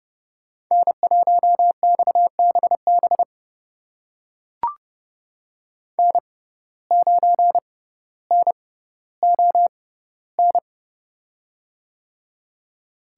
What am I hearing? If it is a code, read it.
Morse code, heard as a keyed tone, about 31 wpm, spaced as for 15 wpm, 700 Hz. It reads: N1XBB N9NON
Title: USA Call Signs - 5 Characters - 6x Character Spacing - 30wpm